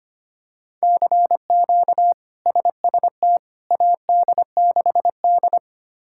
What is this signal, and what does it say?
Morse code, heard as a keyed tone, about 25 wpm, 700 Hz. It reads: CQ SST AD6D
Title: K1USN SST QSO Practice - 25wpm